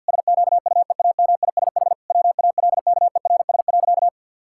Morse code, heard as a keyed tone, about 52 wpm, 700 Hz. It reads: SQUEAMISH WALKERS.
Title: Pangram Challenge v2 - 50wpm